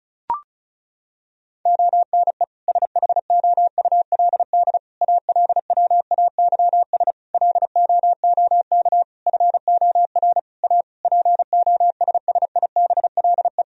Morse code, heard as a keyed tone, about 36 wpm, 700 Hz. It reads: ONE SHOULD ALWAYS LOOK FOR A POSSIBLE
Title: Sentences from Top 1000 Words - Review Part 4 - 35wpm